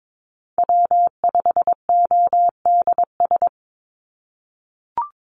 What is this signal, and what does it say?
Morse code, heard as a keyed tone, about 22 wpm, 700 Hz. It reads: W5ODS
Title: NA SOTA Chaser Callsigns 22wpm